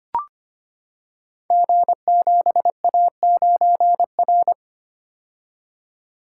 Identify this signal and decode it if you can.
Morse code, heard as a keyed tone, about 25 wpm, 700 Hz. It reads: G7A9R
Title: Code Groups - 5 Letter:Numbers - 25wpm